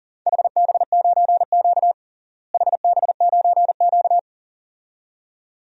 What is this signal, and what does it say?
Morse code, heard as a keyed tone, about 37 wpm, 700 Hz. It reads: HB9Q HB9Q
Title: DXCC-100 Call Signs - 4 Characters - Repeated 2x - 40wpm